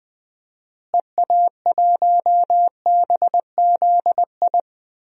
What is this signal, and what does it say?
Morse code, heard as a keyed tone, about 20 wpm, 700 Hz. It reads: EA1BZI
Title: DXCC-100 Call Signs - 6 Characters - Repeated 3x - 20wpm